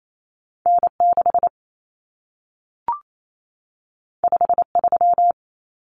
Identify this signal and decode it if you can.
Morse code, heard as a keyed tone, about 28 wpm, 700 Hz. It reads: N6 53
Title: Common Cut Numbers - 2-Digit 28wpm